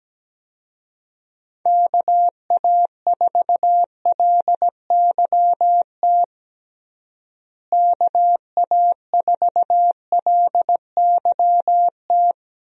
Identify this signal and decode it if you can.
Morse code, heard as a keyed tone, about 17 wpm, 700 Hz. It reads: KA4LYT KA4LYT
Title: USA Call Signs - 6 Characters - Repeated 3x - 17wpm